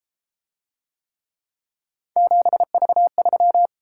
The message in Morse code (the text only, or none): 7V3